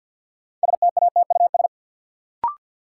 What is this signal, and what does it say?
Morse code, heard as a keyed tone, about 53 wpm, 700 Hz. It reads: STATUS
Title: Sentences from Top 1400 Words - New Word Drill - 50wpm